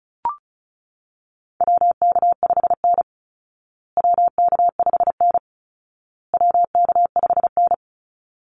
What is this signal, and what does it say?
Morse code, heard as a keyed tone, about 34 wpm, 700 Hz. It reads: WK5N WK5N WK5N
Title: Call Signs Repeated 2x - 35wpm